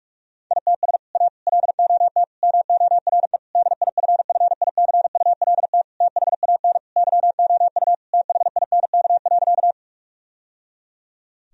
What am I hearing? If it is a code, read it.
Morse code, heard as a keyed tone, about 46 wpm, 700 Hz. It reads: ITS A LOT MORE DIFFICULT THAN YOU THINK.
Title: Sentences from Top 1000 Words - Review Part 3 - 45wpm